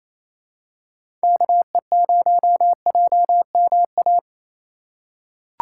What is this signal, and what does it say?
Morse code, heard as a keyed tone, about 28 wpm, 690 Hz. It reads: KE0JMA
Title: USA Call Signs - 6 Characters - 4x Character Spacing - 28wpm